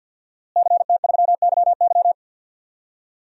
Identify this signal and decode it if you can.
Morse code, heard as a keyed tone, about 52 wpm, 700 Hz. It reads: CT3YY
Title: DXCC-100 Call Signs - 5 Characters - Repeated 3x - 50wpm